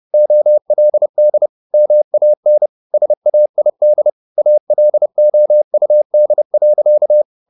Morse code, heard as a keyed tone, about 30 wpm, 600 Hz. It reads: OLD MAN SAID ALOUD.
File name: the old man and the sea / the old man and the sea 1 30wpm